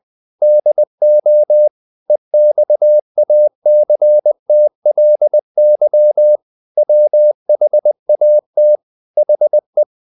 Morse code, heard as a keyed tone, about 20 wpm, 600 Hz. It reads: DO EXACTLY WHAT HE
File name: the old man and the sea part 2 20wpm